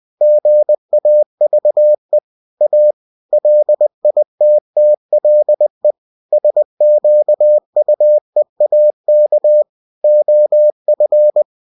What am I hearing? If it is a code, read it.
Morse code, heard as a keyed tone, about 20 wpm, 600 Hz. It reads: GAVE A LITTLE SQUEAK OF
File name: winnie-20wpm